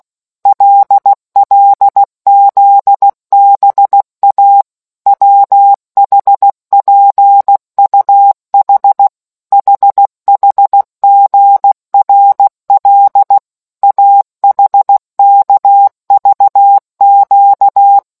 Morse code, heard as a keyed tone, about 16 wpm, 800 Hz. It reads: LLZBA WHPUH HHGRL AHKVQ